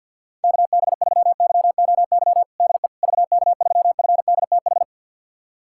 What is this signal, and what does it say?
Morse code, heard as a keyed tone, about 50 wpm, 700 Hz. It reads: KB3YOY DE VK3FDTH
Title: USA Callsign DE Callsign 50wpm